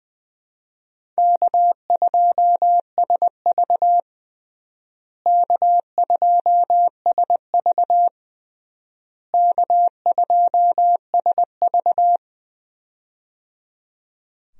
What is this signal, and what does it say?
Morse code, heard as a keyed tone, about 20 wpm, 700 Hz. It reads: K2SV K2SV K2SV
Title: USA Call Signs - 4 Characters - Repeated 3x - 20wpm